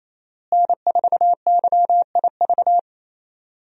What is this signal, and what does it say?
Morse code, heard as a keyed tone, about 28 wpm, 700 Hz. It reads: N4YIV
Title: USA Call Signs - 5 Characters - Repeated 2x - 28wpm